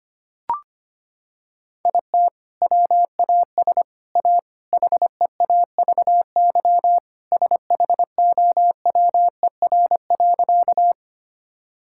Morse code, heard as a keyed tone, about 25 wpm, 700 Hz. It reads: IT WAS A HEAVY SHOWER.
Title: Sentences from Top 1600 Words 25wpm